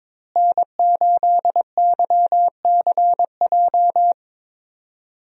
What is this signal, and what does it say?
Morse code, heard as a keyed tone, about 22 wpm, 700 Hz. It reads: N8YCJ
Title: USA Call Signs - 5 Characters - Repeated 3x - 22wpm